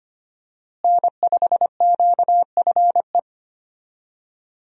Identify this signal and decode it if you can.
Morse code, heard as a keyed tone, about 25 wpm, 700 Hz. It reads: N5QFE
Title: USA Call Signs - 5 Characters - Repeated 3x - 25wpm